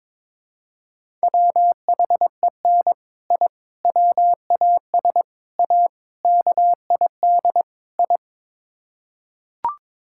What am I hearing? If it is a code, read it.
Morse code, heard as a keyed tone, about 22 wpm, 700 Hz. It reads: WHEN I WAS A KID I
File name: practice / 22wpm / Sets of 6 Words - 22wpm